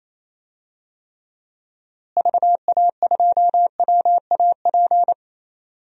Morse code, heard as a keyed tone, about 28 wpm, 700 Hz. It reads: VA2WAP
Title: DXCC-100 Call Signs - 6 Characters - Repeated 2x - 28wpm